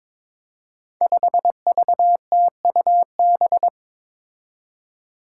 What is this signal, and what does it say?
Morse code, heard as a keyed tone, about 22 wpm, 700 Hz. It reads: 5VTUB
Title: Five Challenging Letter-Number Combinations 22wpm